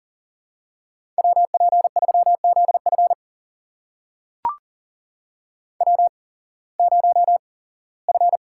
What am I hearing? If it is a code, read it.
Morse code, heard as a keyed tone, about 39 wpm, spaced as for 21 wpm, 700 Hz. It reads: WP3ZF W0F
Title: USA Call Signs - 5 Characters - 8x Character Spacing - 40wpm